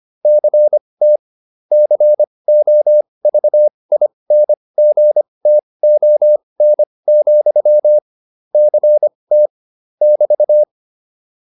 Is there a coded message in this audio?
Yes: Morse code from 0.2 to 10.6 s, heard as a keyed tone, about 25 wpm, spaced as for 20 wpm, 600 Hz. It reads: CT COVINGTON, CT <BT>